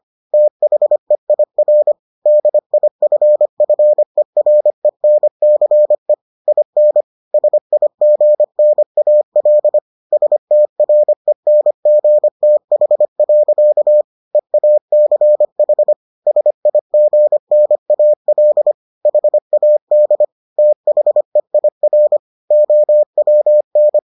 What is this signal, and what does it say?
Morse code, heard as a keyed tone, about 25 wpm, 600 Hz. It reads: THEIR DIFFERENCE IN SIGNAL STRENGTH. EACH SIGNAL HAD THEIR OWN